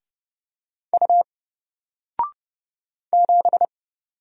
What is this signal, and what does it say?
Morse code, heard as a keyed tone, about 30 wpm, 700 Hz. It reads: U 7
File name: Single-Letter-Number-30wpm